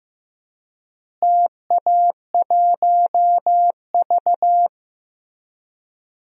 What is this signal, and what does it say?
Morse code, heard as a keyed tone, about 15 wpm, 700 Hz. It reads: TA1V
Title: DXCC-100 Call Signs - 4 Characters - Repeated 3x - 15wpm